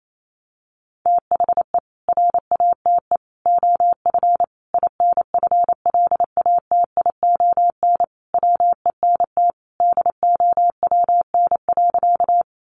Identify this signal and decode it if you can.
Morse code, heard as a keyed tone, about 28 wpm, 700 Hz. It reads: THE RATE OF INFLATION WENT DOWN.
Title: Sentences from Top 1100 Words 28wpm